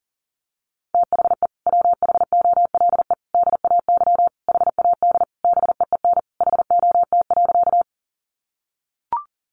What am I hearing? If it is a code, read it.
Morse code, heard as a keyed tone, about 40 wpm, 700 Hz. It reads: THE WHOLE DAY HAD BEEN HOT.
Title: Sentences from Top 500 Words 40wpm